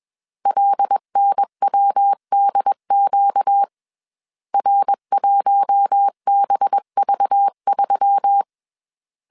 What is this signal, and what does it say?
Morse code, heard as a keyed tone, about 21 wpm, 800 Hz. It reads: LNWDQ R1BV3
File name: CW / DET / 1 6 1500C S A 20wpm 2048